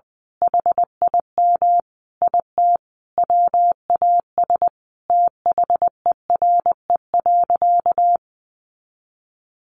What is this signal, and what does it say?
Morse code, heard as a keyed tone, about 20 wpm, 700 Hz. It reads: HIM IT WAS THERE.
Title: A Sound Of Thunder 20wpm